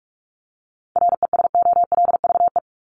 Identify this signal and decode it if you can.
Morse code, heard as a keyed tone, about 44 wpm, 700 Hz. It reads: RESOLVE
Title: Top 1751-2000 Words 45wpm